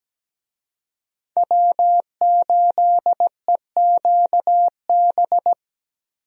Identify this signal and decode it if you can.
Morse code, heard as a keyed tone, about 17 wpm, 700 Hz. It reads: W8EQB
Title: USA Call Signs - 5 Characters - Repeated 2x - 17wpm